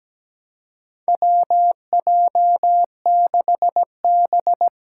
Morse code, heard as a keyed tone, about 17 wpm, 700 Hz. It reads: WJ6B
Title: Four Challenging Letter-Number Combinations 17wpm